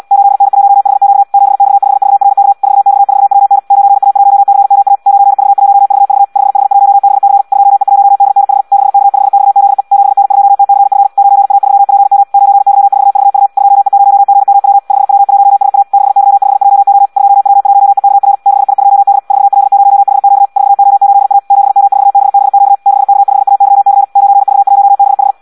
Morse code, heard as a keyed tone, about 74 wpm, 800 Hz. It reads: 8N1HQ 7X5VRK 4X4WN 9A1CMA 9V1VV 5H1BC 2E1DNH 6K5XZE 7N2TCF 9A3PM 9Y4VU 2E0GMW 5W1SA 6Y5JO 2W1EKR 6I2D 4L0HQ 4O8A 7M4BBQ 6K5AYC 9V1BH
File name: Audio practica CW / qrq-tx-74 2